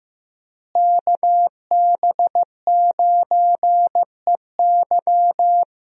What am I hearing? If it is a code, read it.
Morse code, heard as a keyed tone, about 15 wpm, 700 Hz. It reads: KB9EY
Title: USA Call Signs - 5 Characters - Repeated 2x - 15wpm